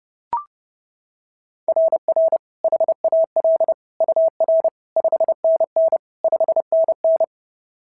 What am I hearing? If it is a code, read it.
Morse code, heard as a keyed tone, about 30 wpm, 690 Hz. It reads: RR HAL UR 5NN 5NN
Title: SOTA QSO Practice 30wpm